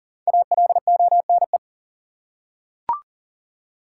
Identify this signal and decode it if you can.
Morse code, heard as a keyed tone, about 40 wpm, 700 Hz. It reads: ALONE
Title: Five Letter Words - 40wpm